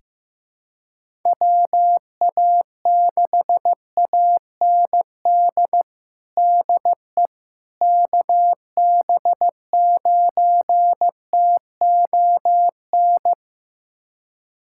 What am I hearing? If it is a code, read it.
Morse code, heard as a keyed tone, about 15 wpm, 700 Hz. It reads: WA6AND DE KB9TON
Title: USA Callsign DE Callsign 15wpm